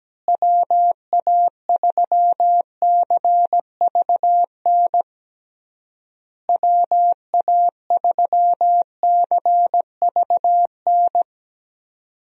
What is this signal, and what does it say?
Morse code, heard as a keyed tone, about 17 wpm, 700 Hz. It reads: WA3CVN WA3CVN
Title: USA Call Signs - 6 Characters - Repeated 3x - 17wpm